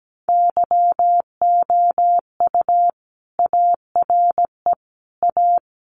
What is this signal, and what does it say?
Morse code, heard as a keyed tone, about 17 wpm, 700 Hz. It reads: YOU ARE A
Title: Sentences from Top 200 Words 17wpm